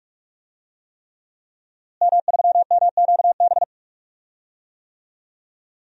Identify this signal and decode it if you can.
Morse code, heard as a keyed tone, about 46 wpm, 700 Hz. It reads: M3MQB
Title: DXCC-100 Call Signs - 5 Characters - Repeated 3x - 45wpm